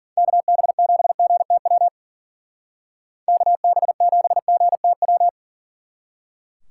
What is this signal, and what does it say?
Morse code, heard as a keyed tone, about 50 wpm, 700 Hz. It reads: KB7GTW KB7GTW
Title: Speed Racing - USA 2x3 Call Signs 40wpm